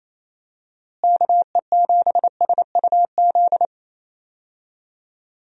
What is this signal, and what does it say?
Morse code, heard as a keyed tone, about 27 wpm, 700 Hz. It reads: KE7SUZ
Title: USA Call Signs - 6 Characters - Repeated 2x - 28wpm